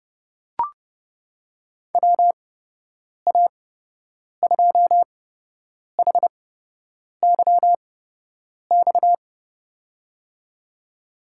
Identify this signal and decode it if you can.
Morse code, heard as a keyed tone, about 32 wpm, spaced as for 8 wpm, 700 Hz. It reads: WA2HYX
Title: USA Call Signs - 8x Character Spacing - 30wpm